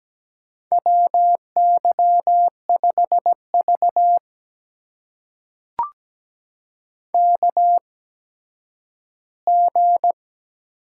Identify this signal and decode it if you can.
Morse code, heard as a keyed tone, about 17 wpm, spaced as for 10 wpm, 700 Hz. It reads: WY5V KG
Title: USA Call Signs - 4 Characters - 8x Character Spacing - 17wpm